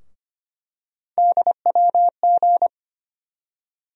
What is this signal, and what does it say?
Morse code, heard as a keyed tone, about 25 wpm, 700 Hz. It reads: DWG
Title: USA Call Sign Postfixes - 3 Letters 25wpm